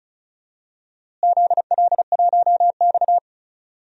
Morse code, heard as a keyed tone, about 34 wpm, 700 Hz. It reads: ZL1X
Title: DXCC-100 Call Signs - 4 Characters - Repeated 2x - 35wpm